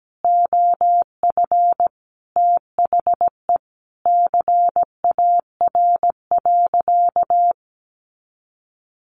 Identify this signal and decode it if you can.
Morse code, heard as a keyed tone, about 17 wpm, 700 Hz. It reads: OF THE CAR.
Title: Easy Phrases 17wpm